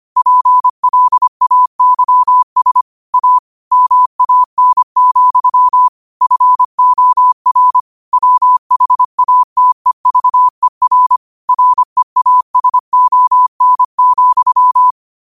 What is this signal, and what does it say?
Morse code, heard as a keyed tone, about 25 wpm, 1000 Hz. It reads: PLAYS A MAN, FOR WHATEVER REASON,
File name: qotdfiles / 25wpm